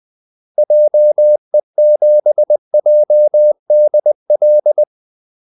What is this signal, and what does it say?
Morse code, heard as a keyed tone, about 20 wpm, 600 Hz. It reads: JE7JDL
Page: Kaldesignaler 20 wpm